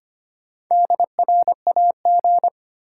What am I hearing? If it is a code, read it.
Morse code, heard as a keyed tone, about 25 wpm, 700 Hz. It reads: DRAG
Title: Sentences from Top 1500 Words - New Word Drill Alt - 25wpm